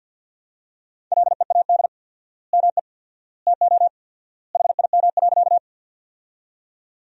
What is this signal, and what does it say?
Morse code, heard as a keyed tone, about 50 wpm, 700 Hz. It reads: LEAD ME TO HIM.
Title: 4x Word-Spacing - Sentences from Top 500 Words - 50wpm